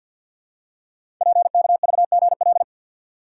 Morse code, heard as a keyed tone, about 48 wpm, 700 Hz. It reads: PKVGL
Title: Code Groups - 5 Letters - 50wpm